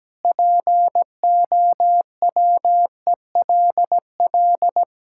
Morse code, heard as a keyed tone, about 17 wpm, 700 Hz. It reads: POWELL
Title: Pangram Challenge v2 - 17wpm